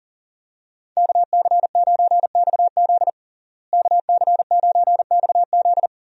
Morse code, heard as a keyed tone, about 37 wpm, 700 Hz. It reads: KC9XZ KC9XZ
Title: USA Call Signs - 5 Characters - Repeated 2x - 40wpm